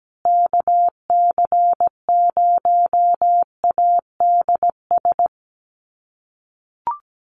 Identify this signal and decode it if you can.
Morse code, heard as a keyed tone, about 17 wpm, 700 Hz. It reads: KC0ADS
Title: USA Call Signs Repeated 2x 17wpm